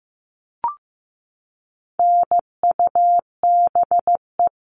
Morse code, heard as a keyed tone, about 15 wpm, 700 Hz. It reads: NUBE
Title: All Cut Numbers - 4-Digit 15wpm